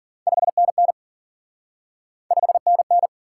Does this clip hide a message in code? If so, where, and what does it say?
Yes: Morse code from 0.3 to 3.1 s, heard as a keyed tone, about 52 wpm, 700 Hz. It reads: 5NN 5NN